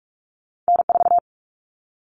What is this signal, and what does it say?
Morse code, heard as a keyed tone, about 46 wpm, 700 Hz. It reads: N4
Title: All Cut Numbers - 2-Digit 45wpm